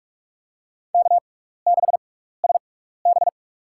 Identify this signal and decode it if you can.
Morse code, heard as a keyed tone, about 42 wpm, spaced as for 15 wpm, 700 Hz. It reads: K6SB
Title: USA Call Signs - 4 Characters - 6x Character Spacing - 45wpm